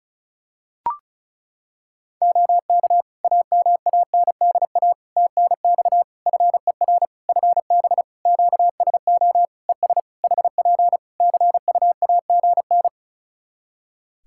Morse code, heard as a keyed tone, about 36 wpm, 700 Hz. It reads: OK AMANDA TNX FER FB QSO ES HP CUAGN
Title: K1USN SST QSO Practice 35wpm